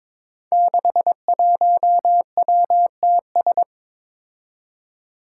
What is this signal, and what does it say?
Morse code, heard as a keyed tone, about 22 wpm, 700 Hz. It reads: 61WTS
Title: Five Challenging Letter-Number Combinations 22wpm